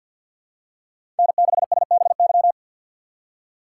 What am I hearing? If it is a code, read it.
Morse code, heard as a keyed tone, about 54 wpm, 700 Hz. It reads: N6SBY
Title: USA Call Signs - 5 Characters - 8x Character Spacing - 50wpm